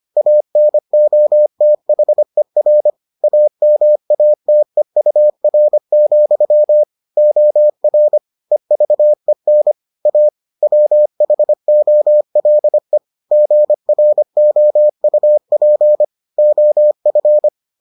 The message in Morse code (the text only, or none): ANOTHER AMATEUR, OR EVEN A WHOLE GROUP OF